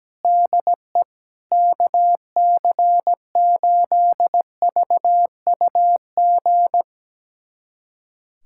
Morse code, heard as a keyed tone, about 17 wpm, 700 Hz. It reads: DE KC8VUG
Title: USA Callsign DE Callsign - 6 Characters - 17wpm